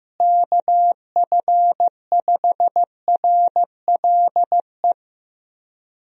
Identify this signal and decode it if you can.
Morse code, heard as a keyed tone, about 15 wpm, 700 Hz. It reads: KF5RLE
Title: USA Call Signs - 6 Characters - Repeated 3x - 15wpm